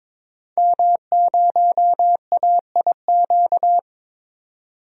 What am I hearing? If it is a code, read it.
Morse code, heard as a keyed tone, about 22 wpm, 700 Hz. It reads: M0AIQ
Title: DXCC-100 Call Signs - 5 Characters - 22wpm